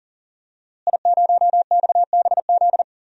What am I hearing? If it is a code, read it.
Morse code, heard as a keyed tone, about 37 wpm, 700 Hz. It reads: I0XBZ
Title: DXCC-100 Call Signs - 5 Characters - Repeated 2x - 40wpm